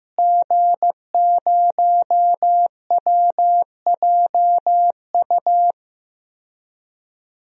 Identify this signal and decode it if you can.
Morse code, heard as a keyed tone, about 15 wpm, 700 Hz. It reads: G0WJU
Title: DXCC-100 Call Signs - 5 Characters - Repeated 3x - 15wpm